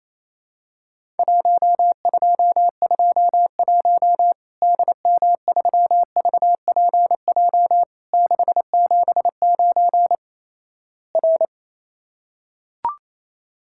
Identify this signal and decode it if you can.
Morse code, heard as a keyed tone, about 28 wpm, 700 Hz. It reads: 1221 DM3VPJ 679 R
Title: Worked All Europe DX Contest - QTC Exchange - 28wpm